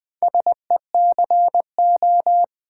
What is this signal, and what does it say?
Morse code, heard as a keyed tone, about 20 wpm, 700 Hz. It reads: SECO
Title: Build A Word - 4 Letters At A Time - 8-Letter Words - 20wpm